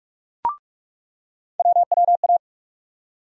Morse code, heard as a keyed tone, about 46 wpm, 700 Hz. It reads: WWA
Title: USA Call Sign Postfixes - 3 Letters 45wpm